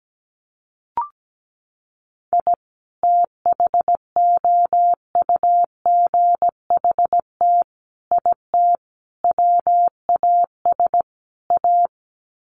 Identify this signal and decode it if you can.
Morse code, heard as a keyed tone, about 17 wpm, 700 Hz. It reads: I THOUGHT IT WAS A
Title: Sets of 5 Words 17wpm